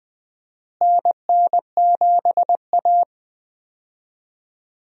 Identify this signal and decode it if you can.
Morse code, heard as a keyed tone, about 20 wpm, 700 Hz. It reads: NN7A
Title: USA Call Signs - 4 Characters - Repeated 2x - 20wpm